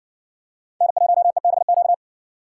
Morse code, heard as a keyed tone, about 60 wpm, 700 Hz. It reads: N1EBQ